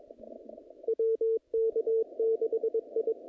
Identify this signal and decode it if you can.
Morse code, heard as a keyed tone, about 23 wpm, 450 Hz, with radio noise. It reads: WK6I
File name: contests / naqp-cw-k5tr-op-k5pi- -0000